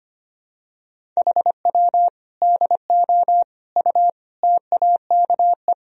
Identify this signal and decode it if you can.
Morse code, heard as a keyed tone, about 25 wpm, 700 Hz. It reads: HW DO U TAKE
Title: Sentences from Top 200 Words and Abbreviations 25wpm